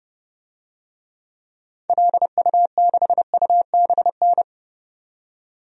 Morse code, heard as a keyed tone, about 30 wpm, 700 Hz. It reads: LU6UBN
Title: DXCC-100 Call Signs - 6 Characters - 30wpm